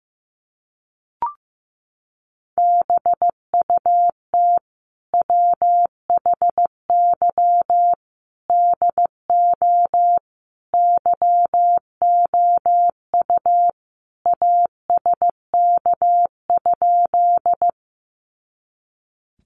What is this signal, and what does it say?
Morse code, heard as a keyed tone, about 15 wpm, 700 Hz. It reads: BUT WHY DO YOU ASK?
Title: Sentences from Top 200 Words 15wpm